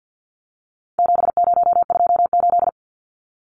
Morse code, heard as a keyed tone, about 50 wpm, 700 Hz. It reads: 7028
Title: Numbers - 4-Digit 50wpm